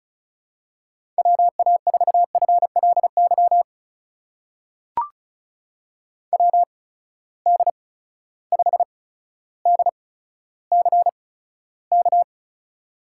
Morse code, heard as a keyed tone, about 36 wpm, spaced as for 15 wpm, 700 Hz. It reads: WA4FLY WD5DCK
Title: USA Call Signs - 6 Characters - 8x Character Spacing - 35wpm